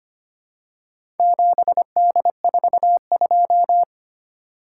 Morse code, heard as a keyed tone, about 25 wpm, 690 Hz. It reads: 7D42
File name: practice / 25wpm / Four Letter-Number Combinations - 25wpm